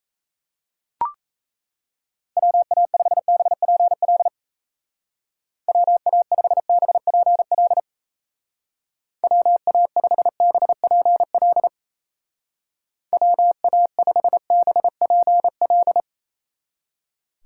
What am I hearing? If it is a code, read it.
Morse code, heard as a keyed tone, about 34 wpm, 700 Hz. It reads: WA5BPL WA5BPL WA5BPL WA5BPL
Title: Speed Racing - USA Call Signs 28wpm